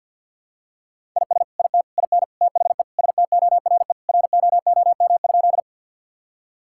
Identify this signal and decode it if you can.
Morse code, heard as a keyed tone, about 50 wpm, 700 Hz. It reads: IS IT IN THE STORE ROOM?
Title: Sentences from Top 1000 Words - Review Part 5 - 50wpm